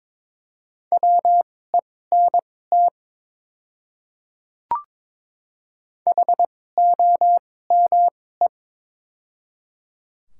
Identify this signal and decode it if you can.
Morse code, heard as a keyed tone, about 22 wpm, spaced as for 16 wpm, 700 Hz. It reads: WENT HOME
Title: 2x Character Spacing - Top 200 Words - 22wpm